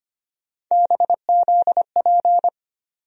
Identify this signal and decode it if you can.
Morse code, heard as a keyed tone, about 25 wpm, 700 Hz. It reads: BZP
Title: USA Call Sign Postfixes - 3 Letters 25wpm